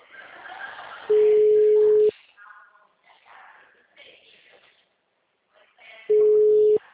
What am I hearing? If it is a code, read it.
Ringback tone (Europe, 425 Hz).
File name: out- -2041- - - .12942